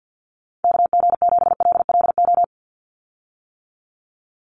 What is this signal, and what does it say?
Morse code, heard as a keyed tone, about 50 wpm, 700 Hz. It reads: KG7LLO
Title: USA Call Signs 50wpm